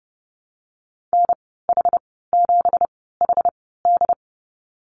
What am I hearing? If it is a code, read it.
Morse code, heard as a keyed tone, about 30 wpm, spaced as for 17 wpm, 700 Hz. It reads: NH7HD
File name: Lesson 37g - Letter x - Call Signs 30wpm